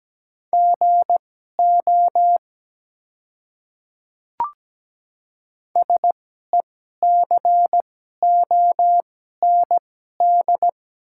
Morse code, heard as a keyed tone, about 17 wpm, spaced as for 12 wpm, 700 Hz. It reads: GO SECOND
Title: 2x Character Spacing - Top 300 Words - 17wpm